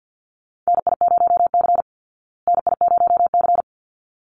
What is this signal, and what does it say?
Morse code, heard as a keyed tone, about 49 wpm, 700 Hz. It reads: NI0C NI0C
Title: Call Signs Repeated 3x - 50wpm